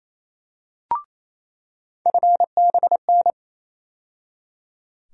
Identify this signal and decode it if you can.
Morse code, heard as a keyed tone, about 28 wpm, 700 Hz. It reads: FBN
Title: USA Call Sign Postfixes - 3 Letters 28wpm